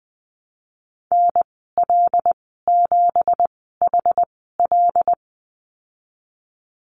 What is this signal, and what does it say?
Morse code, heard as a keyed tone, about 19 wpm, spaced as for 14 wpm, 700 Hz. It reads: NL7HL
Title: 173 Lesson 28g - Number 9 - Call Signs 20wpm